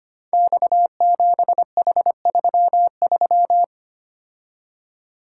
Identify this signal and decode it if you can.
Morse code, heard as a keyed tone, about 25 wpm, 700 Hz. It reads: X7H33
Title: Code Groups - 5 Letter:Numbers - 25wpm